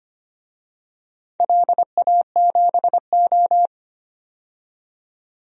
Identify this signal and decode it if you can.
Morse code, heard as a keyed tone, about 25 wpm, 700 Hz. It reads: LA7O
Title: DXCC-100 Call Signs - 4 Characters - 25wpm